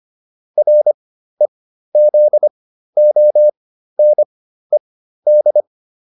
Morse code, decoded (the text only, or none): REZONED